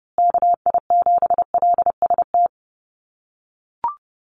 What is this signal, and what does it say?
Morse code, heard as a keyed tone, about 30 wpm, 700 Hz. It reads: KI7LST
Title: USA Call Signs Repeated 2x 30wpm